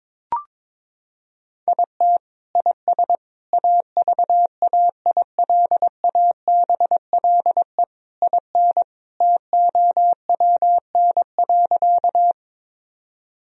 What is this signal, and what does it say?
Morse code, heard as a keyed tone, about 22 wpm, 700 Hz. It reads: IT IS AVAILABLE IN TOWN.
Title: Sentences from Top 1000 Words - Review Part 3 - 22wpm